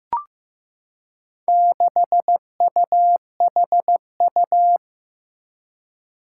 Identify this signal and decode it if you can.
Morse code, heard as a keyed tone, about 15 wpm, 700 Hz. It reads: 6UHU
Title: Four Challenging Letter-Number Combinations 15wpm